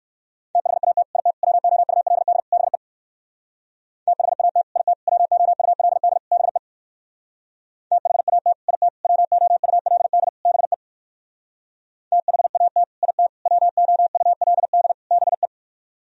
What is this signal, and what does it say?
Morse code, heard as a keyed tone, about 58 wpm, 700 Hz. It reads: THAT IT WOULD BE THAT IT WOULD BE THAT IT WOULD BE THAT IT WOULD BE